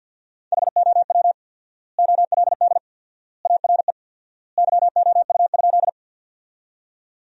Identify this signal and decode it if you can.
Morse code, heard as a keyed tone, about 47 wpm, 700 Hz. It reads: HOW OLD ARE YOU?
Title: 4x Word-Spacing - Sentences from Top 200 Words - 50wpm